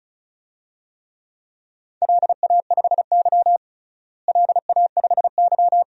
Morse code, heard as a keyed tone, about 36 wpm, 700 Hz. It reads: LA5Y LA5Y
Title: DXCC-100 Call Signs - 4 Characters - Repeated 2x - 35wpm